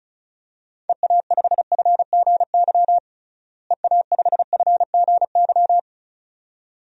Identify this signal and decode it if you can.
Morse code, heard as a keyed tone, about 36 wpm, 700 Hz. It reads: EA5FGY EA5FGY
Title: DXCC-100 Call Signs - 6 Characters - Repeated 2x - 35wpm